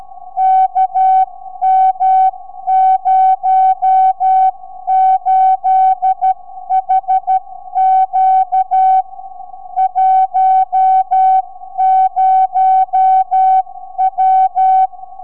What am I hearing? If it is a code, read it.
Morse code, heard as a keyed tone, about 12 wpm, 760 Hz. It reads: KM08HQ 10W
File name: sv8four 4m